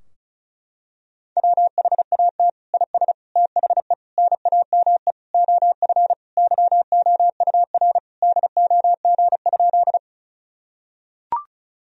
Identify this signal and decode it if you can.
Morse code, heard as a keyed tone, about 36 wpm, 700 Hz. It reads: WHAT IS THE NAME OF YOUR DOG?
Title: Sentences from Top 1000 Words - Review Part 2 - 35wpm